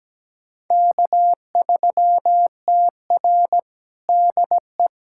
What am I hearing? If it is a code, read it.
Morse code, heard as a keyed tone, about 17 wpm, 700 Hz. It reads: K3TR DE
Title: USA Callsign DE Callsign - 4 Characters - 17wpm